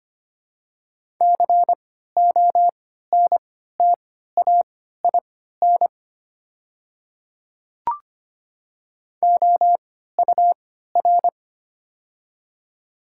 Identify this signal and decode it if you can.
Morse code, heard as a keyed tone, about 25 wpm, spaced as for 14 wpm, 700 Hz. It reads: CONTAIN OUR
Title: 3x Character Spacing - Top 500 Words - 25wpm